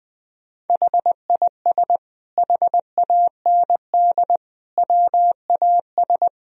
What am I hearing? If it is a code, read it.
Morse code, heard as a keyed tone, about 21 wpm, 700 Hz. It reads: HIS HAND WAS
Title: Sentences from Top 1000 Words - Review Part 1 - 20wpm